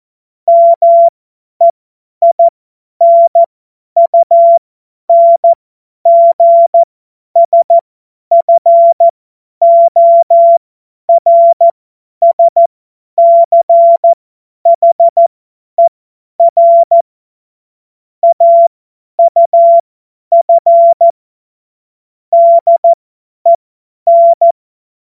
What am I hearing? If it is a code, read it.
Morse code, heard as a keyed tone, about 14 wpm, spaced as for 10 wpm, 680 Hz. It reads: MEINUNGSFORSCHER AUF DEN
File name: Morsekurs / Witze / Witz 11. Januar 2022, 50 F70